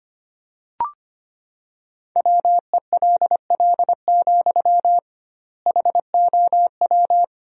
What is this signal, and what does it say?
Morse code, heard as a keyed tone, about 25 wpm, 700 Hz. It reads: WELL, HOW
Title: 2x Word-Spacing - Sentences from Top 200 Words - 25wpm